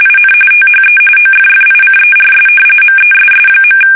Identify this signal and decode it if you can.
RTTY, about 45.45 baud. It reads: IN THE OFFICERS MESS.
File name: fsk850